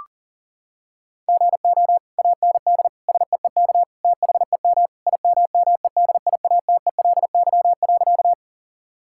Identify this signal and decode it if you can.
Morse code, heard as a keyed tone, about 40 wpm, 700 Hz. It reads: GO AND SEEK THEM IMMEDIATELY.
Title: Sentences from Top 1600 Words 40wpm